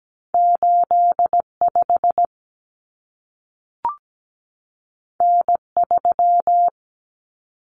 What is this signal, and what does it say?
Morse code, heard as a keyed tone, about 17 wpm, 700 Hz. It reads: 85 N3
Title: Common Cut Numbers - 2-Digit 17wpm